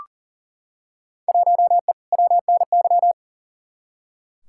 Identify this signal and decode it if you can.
Morse code, heard as a keyed tone, about 40 wpm, 700 Hz. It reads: 1E WNY
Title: Field Day Exchanges - 40wpm